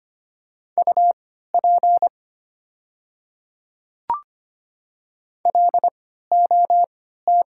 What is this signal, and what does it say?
Morse code, heard as a keyed tone, about 25 wpm, spaced as for 14 wpm, 700 Hz. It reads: UP LOT
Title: 3x Character Spacing - Top 500 Words - 25wpm